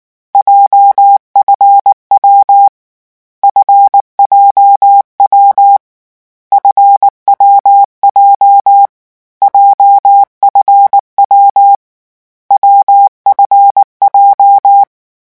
Morse code, heard as a keyed tone, about 19 wpm, 800 Hz. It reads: JFW FJW FWJ JFW WFJ
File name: morse mp3 / lecon7-vitesse 18